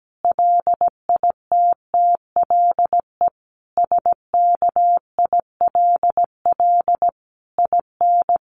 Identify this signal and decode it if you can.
Morse code, heard as a keyed tone, about 17 wpm, 700 Hz. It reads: LITTLE SKILL IN
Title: Sentences from Top 1200 Words 17wpm